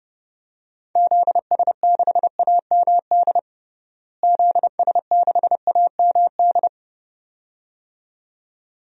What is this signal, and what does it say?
Morse code, heard as a keyed tone, about 32 wpm, 700 Hz. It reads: ZS6AMD ZS6AMD
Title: DXCC-100 Call Signs - 6 Characters - Repeated 2x - 30wpm